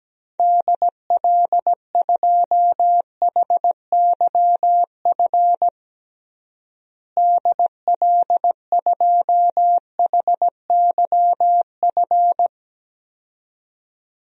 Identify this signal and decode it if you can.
Morse code, heard as a keyed tone, about 17 wpm, 700 Hz. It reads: DL2HYF DL2HYF
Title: DXCC-100 Call Signs - 6 Characters - Repeated 2x - 17wpm